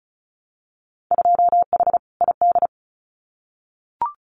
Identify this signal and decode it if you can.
Morse code, heard as a keyed tone, about 34 wpm, 700 Hz. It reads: 2H ID
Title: Winter Field Day Exchanges 35wpm